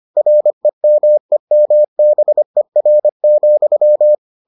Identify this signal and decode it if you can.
Morse code, heard as a keyed tone, about 25 wpm, 600 Hz. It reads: REMEMBER,